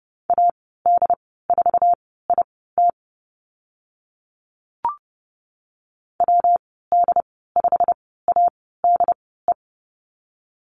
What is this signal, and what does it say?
Morse code, heard as a keyed tone, about 30 wpm, spaced as for 17 wpm, 700 Hz. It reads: AD4IT WD5ADE
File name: Lesson 19g - Letter w - Call Signs 30wpm